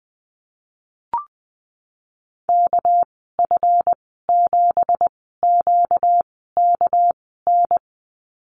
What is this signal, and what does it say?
Morse code, heard as a keyed tone, about 20 wpm, spaced as for 15 wpm, 700 Hz. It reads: KF7QKN
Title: 245 Lesson 39g - Letter z - Call Signs 20wpm